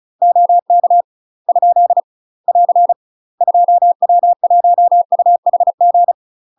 Morse code, heard as a keyed tone, about 34 wpm, 700 Hz. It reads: OK ? <AR> 2W1UHG
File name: UK MASTER QSOs TRAINING / 35WPM / UK MASTER QSOs 0042